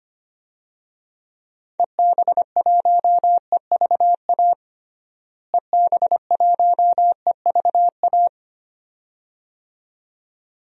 Morse code, heard as a keyed tone, about 25 wpm, 700 Hz. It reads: EB1EVA EB1EVA
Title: DXCC-100 Call Signs - 6 Characters - Repeated 2x - 25wpm